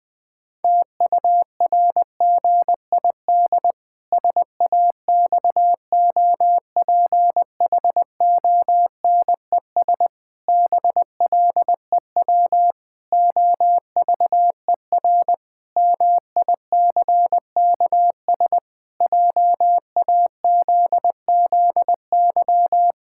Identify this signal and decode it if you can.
Morse code, heard as a keyed tone, about 20 wpm, 700 Hz. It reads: TURGID SAXOPHONES BLEW OVER MICKS JAZZY
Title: Pangram Challenge v2 - 20wpm